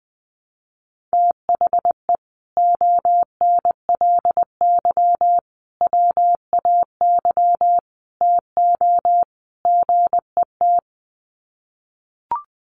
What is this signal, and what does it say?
Morse code, heard as a keyed tone, about 20 wpm, 700 Hz. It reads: THE ONLY WAY TO GET
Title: Sets of 5 Words Encore2 20wpm